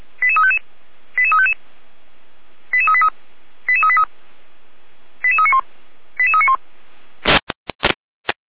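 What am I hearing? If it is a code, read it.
Five-tone selective call (ZVEI), heard as quick runs of five tones, tones of about 70 ms. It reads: 88355 88355 88383 88383 88391 88391
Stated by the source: ZVEI five-tone call 88367 received on SDR.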